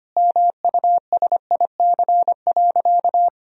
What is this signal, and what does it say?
Morse code, heard as a keyed tone, about 25 wpm, 700 Hz. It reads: MUSIC.
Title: Sentences from Top 1000 Words - Review Part 1 - 25wpm